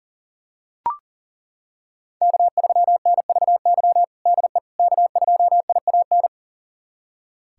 Morse code, heard as a keyed tone, about 40 wpm, 700 Hz. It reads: K3NVY DE K2IAN
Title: USA Callsign DE Callsign - 5 Characters - 40wpm